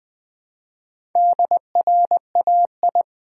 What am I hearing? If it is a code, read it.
Morse code, heard as a keyed tone, about 20 wpm, 700 Hz. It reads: DRAI
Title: Build A Word - 1 Letter At A Time - 5-Letter Words - 20wpm